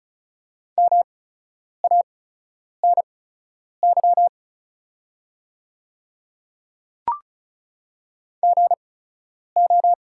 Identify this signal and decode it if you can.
Morse code, heard as a keyed tone, about 35 wpm, spaced as for 10 wpm, 690 Hz. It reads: MANY GO
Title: 8x Character Spacing - Top 100 Words - 35wpm